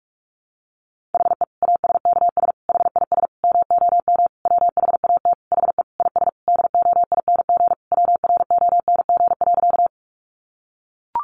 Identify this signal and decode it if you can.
Morse code, heard as a keyed tone, about 44 wpm, 700 Hz. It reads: HE ASKS HIS MOM WHAT HE IS DOING WRONG.
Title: Easy Phrases 45wpm